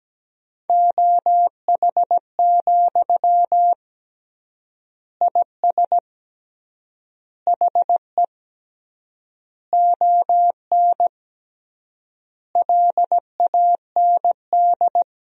Morse code, heard as a keyed tone, about 17 wpm, 700 Hz. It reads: OH, IS HE ON LAND
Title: 3x Word-Spacing - Sentences from Top 500 Words - 17wpm